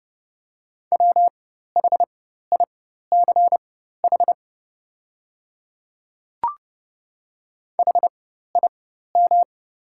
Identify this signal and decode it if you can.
Morse code, heard as a keyed tone, about 30 wpm, spaced as for 14 wpm, 700 Hz. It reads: WHICH HIM
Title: 4x Character Spacing - Top 100 Words - 30wpm